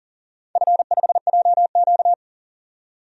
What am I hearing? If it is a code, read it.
Morse code, heard as a keyed tone, about 40 wpm, 700 Hz. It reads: F5JQ
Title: DXCC-100 Call Signs - 4 Characters - 40wpm